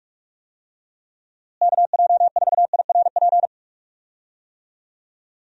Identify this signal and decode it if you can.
Morse code, heard as a keyed tone, about 46 wpm, 700 Hz. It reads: KJ4IRP
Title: USA Call Signs - 6 Characters - Repeated 3x - 45wpm